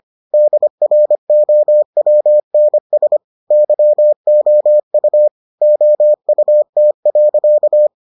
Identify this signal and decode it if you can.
Morse code, heard as a keyed tone, about 25 wpm, 600 Hz. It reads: DROWNS YOU OUT.